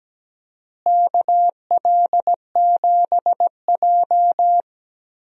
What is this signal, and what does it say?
Morse code, heard as a keyed tone, about 17 wpm, 700 Hz. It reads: KL7J
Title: DXCC-100 Call Signs - 4 Characters - Repeated 3x - 17wpm